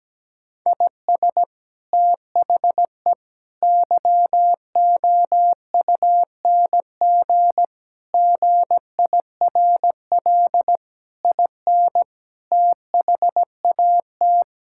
Morse code, heard as a keyed tone, about 17 wpm, 700 Hz. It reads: IS THE YOUNG GIRL IN THAT
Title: Sentences from Top 1000 Words - Review Part 5 - 17wpm